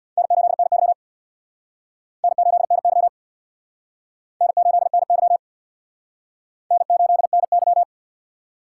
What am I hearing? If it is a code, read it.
Morse code, heard as a keyed tone, about 60 wpm, 700 Hz. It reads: N8NY N8NY N8NY N8NY
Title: Speed Racing - USA 1x2 Call Signs 50wpm